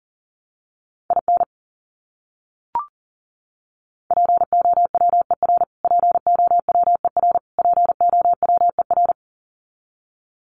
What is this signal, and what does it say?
Morse code, heard as a keyed tone, about 40 wpm, 700 Hz. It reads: IN POWER POWER POWER
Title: Top 500 Words Repeated 40wpm